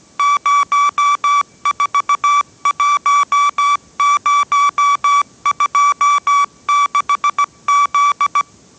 Morse code, heard as a keyed tone, about 17 wpm, 1150 Hz. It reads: 041026Z
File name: 18wpm